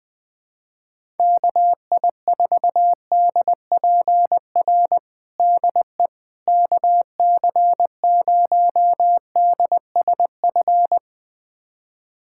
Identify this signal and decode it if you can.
Morse code, heard as a keyed tone, about 20 wpm, 700 Hz. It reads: KI4DPR DE KC0DSF
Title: USA Callsign DE Callsign 20wpm